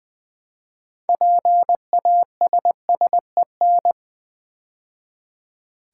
Morse code, heard as a keyed tone, about 21 wpm, 700 Hz. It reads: PASSEN
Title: Build A Word - 3 Letters At A Time - 9-Letter Words - 20wpm